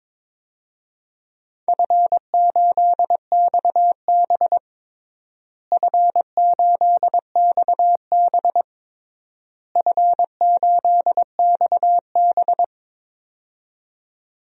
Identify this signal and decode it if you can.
Morse code, heard as a keyed tone, about 22 wpm, 700 Hz. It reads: F8XB F8XB F8XB
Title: DXCC-100 Call Signs - 4 Characters - Repeated 3x - 22wpm